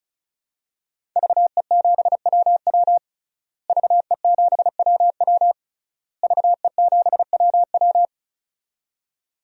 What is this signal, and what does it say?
Morse code, heard as a keyed tone, about 36 wpm, 700 Hz. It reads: VE7WW VE7WW VE7WW
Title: DXCC-100 Call Signs - 5 Characters - Repeated 3x - 35wpm